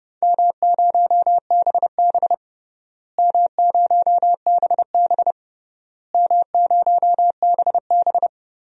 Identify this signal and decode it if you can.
Morse code, heard as a keyed tone, about 32 wpm, 700 Hz. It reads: M0BB M0BB M0BB
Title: DXCC-100 Call Signs - 4 Characters - Repeated 3x - 30wpm